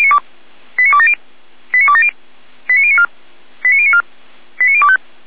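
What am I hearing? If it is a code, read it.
Five-tone selective call (ZVEI), heard as quick runs of five tones, tones of about 70 ms. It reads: 89270 89270 89094 89094 89025
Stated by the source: ZVEI five-tone selective call received by SDR.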